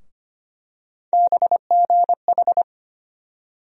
Morse code, heard as a keyed tone, about 25 wpm, 700 Hz. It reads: BGH
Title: USA Call Sign Postfixes - 3 Letters 25wpm